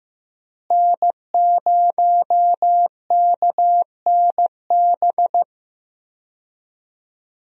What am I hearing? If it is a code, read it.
Morse code, heard as a keyed tone, about 15 wpm, 700 Hz. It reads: N0KNB
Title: USA Call Signs - 5 Characters - Repeated 2x - 15wpm